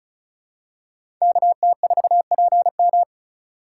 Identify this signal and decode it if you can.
Morse code, heard as a keyed tone, about 36 wpm, 700 Hz. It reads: KT4PM
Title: USA Call Signs - 5 Characters - 3x Character Spacing - 35wpm